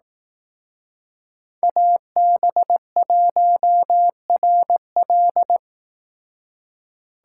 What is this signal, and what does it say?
Morse code, heard as a keyed tone, about 19 wpm, 700 Hz. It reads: AB1RL
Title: Speed Racing - USA Call Signs 15wpm